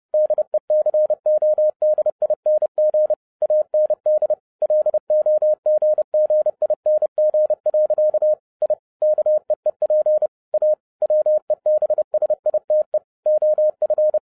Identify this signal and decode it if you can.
Morse code, heard as a keyed tone, about 30 wpm, 610 Hz. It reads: DECODING AND LOGGING. I KEEP A WEBSITE OF